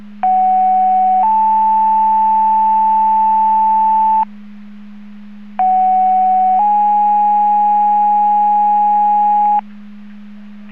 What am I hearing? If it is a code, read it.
Two-tone page, heard as a one-second tone, then a three-second tone, about 750 then 880 Hz, about 750 then 830 Hz. It is heard from 0.2 to 9.6 s.
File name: Portage / 00027 10-33 Portage 154.385 Paging